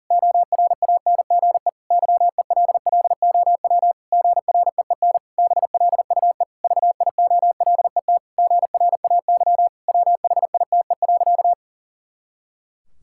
Morse code, heard as a keyed tone, about 37 wpm, 700 Hz. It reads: ORANGE YELLOW GREEN BLUE VIOLET GRAY WHITE.
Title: Pangram Challenge v2 - 40wpm